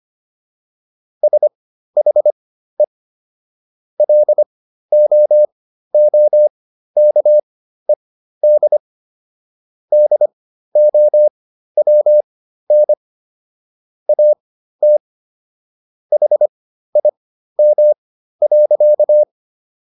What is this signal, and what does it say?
Morse code, heard as a keyed tone, about 25 wpm, spaced as for 13 wpm, 600 Hz. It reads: SHE LOOKED DOWN AT HIM.